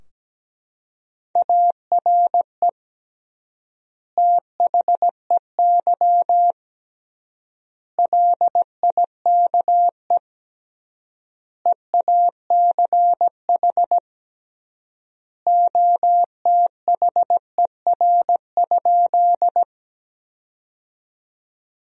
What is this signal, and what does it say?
Morse code, heard as a keyed tone, about 17 wpm, 700 Hz. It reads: ARE THEY LIKE EACH OTHER?
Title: 3x Word-Spacing - Sentences from Top 200 Words - 17wpm